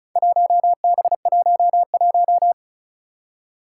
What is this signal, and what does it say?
Morse code, heard as a keyed tone, about 36 wpm, 700 Hz. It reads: 1B11
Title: Four Challenging Letter-Number Combinations 35wpm